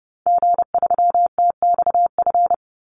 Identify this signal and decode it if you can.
Morse code, heard as a keyed tone, about 30 wpm, 700 Hz. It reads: G3TXF
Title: Call Signs Repeated 3x - 30wpm